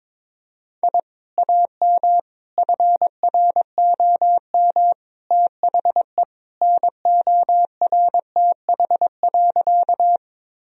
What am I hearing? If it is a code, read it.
Morse code, heard as a keyed tone, about 21 wpm, 700 Hz. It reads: I AM FROM THE NORTH.
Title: Sentences from Top 1000 Words - Review Part 1 - 22wpm